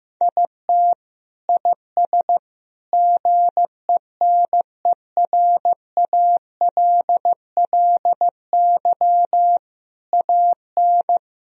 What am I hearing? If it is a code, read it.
Morse code, heard as a keyed tone, about 16 wpm, 700 Hz. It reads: IT IS GENERALLY AN
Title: Sentences from Top 1000 Words - Review Part 4 - 15wpm